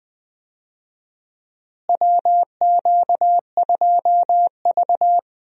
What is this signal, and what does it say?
Morse code, heard as a keyed tone, about 20 wpm, 700 Hz. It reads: WQ2V
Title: USA Call Signs - 4 Characters - Repeated 2x - 20wpm